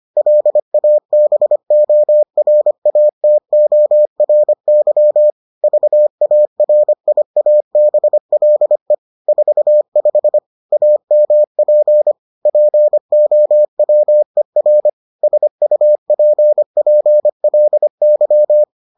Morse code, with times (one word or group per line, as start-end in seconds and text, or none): 0.2-5.3 LABORATORY
5.6-8.9 VARIABLE
9.3-10.4 45
10.7-12.1 AMP
12.5-14.9 POWER
15.2-18.6 SUPPLY